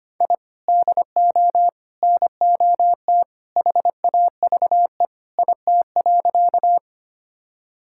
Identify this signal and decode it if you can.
Morse code, heard as a keyed tone, about 25 wpm, 700 Hz. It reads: I DO NOT HAVE IT.
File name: practice / 25wpm / Sentences with 3-Word Phrases - 25wpm